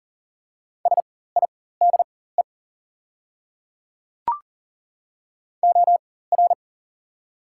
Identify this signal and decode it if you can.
Morse code, heard as a keyed tone, about 40 wpm, spaced as for 19 wpm, 690 Hz. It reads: SIDE OR
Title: 4x Character Spacing - Top 500 Words - 40wpm